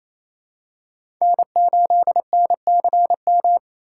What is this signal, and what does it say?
Morse code, heard as a keyed tone, about 27 wpm, 700 Hz. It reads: N8NCM
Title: USA Call Signs - 2x Character Spacing - 28wpm